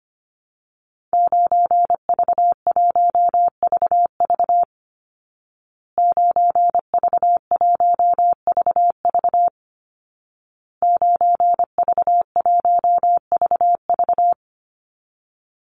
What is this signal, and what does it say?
Morse code, heard as a keyed tone, about 25 wpm, 700 Hz. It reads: 9V1VV 9V1VV 9V1VV
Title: Call Signs Repeated 2x - 25wpm